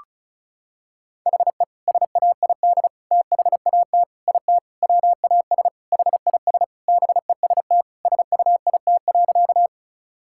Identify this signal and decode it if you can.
Morse code, heard as a keyed tone, about 36 wpm, 700 Hz. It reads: HE SAID THAT IT WAS HIS BEST SUIT.
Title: Sentences from Top 1600 Words 35wpm